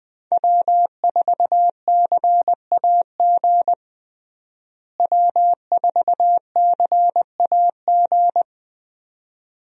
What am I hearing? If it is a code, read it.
Morse code, heard as a keyed tone, about 20 wpm, 700 Hz. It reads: W4CAG W4CAG
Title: USA Call Signs - 5 Characters - Repeated 2x - 20wpm